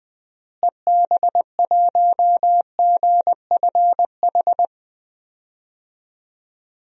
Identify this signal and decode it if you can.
Morse code, heard as a keyed tone, about 20 wpm, 700 Hz. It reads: EB1GFH
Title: DXCC-100 Call Signs - 6 Characters - Repeated 3x - 20wpm